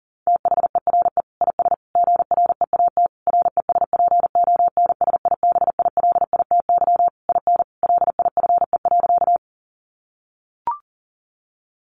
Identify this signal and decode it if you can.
Morse code, heard as a keyed tone, about 40 wpm, 700 Hz. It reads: THERE IS GREAT RESPONSIBILITY IN LIFE.
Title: Sentences from Top 1200 Words 40wpm